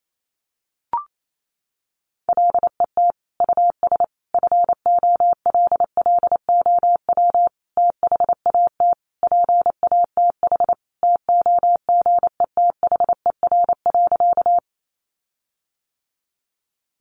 Morse code, heard as a keyed tone, about 28 wpm, 700 Hz. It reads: LET US FOLLOW THAT PATH TOGETHER.
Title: Sentences from Top 1400 Words 28wpm